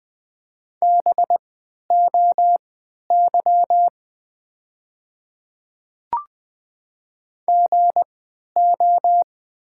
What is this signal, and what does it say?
Morse code, heard as a keyed tone, about 20 wpm, spaced as for 11 wpm, 700 Hz. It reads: BOY GO
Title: 3x Character Spacing - Top 200 Words - 20wpm